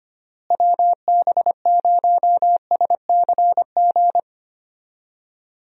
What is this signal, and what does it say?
Morse code, heard as a keyed tone, about 25 wpm, 700 Hz. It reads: WB0SCG
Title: USA Call Signs - 6 Characters - Repeated 2x - 25wpm